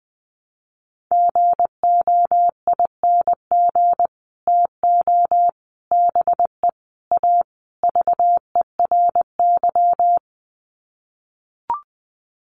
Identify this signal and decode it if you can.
Morse code, heard as a keyed tone, about 20 wpm, 700 Hz. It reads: GOING TO BE A VERY
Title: Sets of 5 Words Encore 20wpm